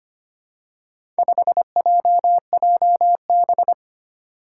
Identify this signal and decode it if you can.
Morse code, heard as a keyed tone, about 25 wpm, 700 Hz. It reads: 5JJB
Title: Four Challenging Letter-Number Combinations 25wpm